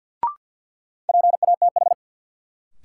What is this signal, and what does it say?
Morse code, heard as a keyed tone, about 52 wpm, 700 Hz. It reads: PATH
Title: Sentences from Top 1400 Words - New Word Drill - 50wpm